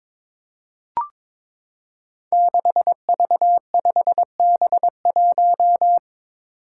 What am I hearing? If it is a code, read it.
Morse code, heard as a keyed tone, about 22 wpm, 700 Hz. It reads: 6V5B1
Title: Five Challenging Letter-Number Combinations 22wpm